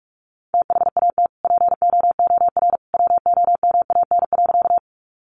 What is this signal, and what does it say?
Morse code, heard as a keyed tone, about 44 wpm, 700 Hz. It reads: THAT POOR WOMAN.
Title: Sentences from Top 1100 Words 45wpm